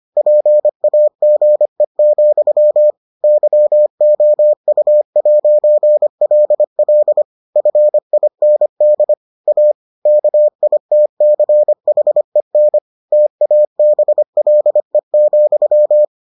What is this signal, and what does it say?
Morse code, heard as a keyed tone, about 25 wpm, 600 Hz. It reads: PAGE, YOU'LL FIND A KITCHEN TABLE,